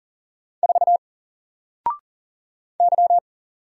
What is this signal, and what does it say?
Morse code, heard as a keyed tone, about 40 wpm, 700 Hz. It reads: 4 Y
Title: Single Letter-Number Alt 40wpm